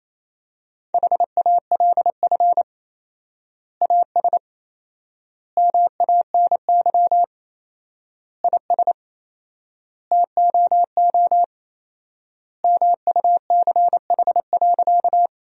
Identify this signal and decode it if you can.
Morse code, heard as a keyed tone, about 28 wpm, 700 Hz. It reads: HALF AS MANY IS TOO MUCH.
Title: 4x Word-Spacing - Sentences from Top 400 Words - 28wpm